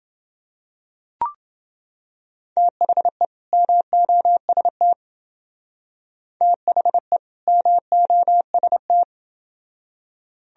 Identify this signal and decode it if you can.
Morse code, heard as a keyed tone, about 28 wpm, 700 Hz. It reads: THE MOST THE MOST